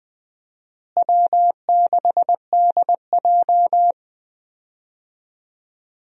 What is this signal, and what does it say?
Morse code, heard as a keyed tone, about 20 wpm, 700 Hz. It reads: W6DJ
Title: USA Call Signs - 4 Characters - Repeated 3x - 20wpm